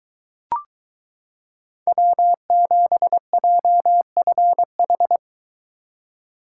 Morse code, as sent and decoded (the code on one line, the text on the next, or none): .-- --... .--- ..-. ....
W7JFH